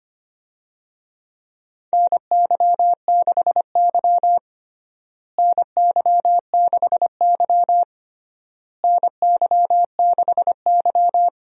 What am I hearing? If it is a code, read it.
Morse code, heard as a keyed tone, about 25 wpm, 700 Hz. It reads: NY6Y NY6Y NY6Y
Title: USA Call Signs - 4 Characters - Repeated 3x - 25wpm